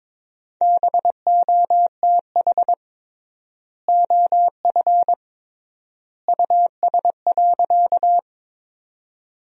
Morse code, heard as a keyed tone, about 22 wpm, 700 Hz. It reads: BOTH OF US.
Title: 3x Word-Spacing - Sentences from Top 300 Words - 22wpm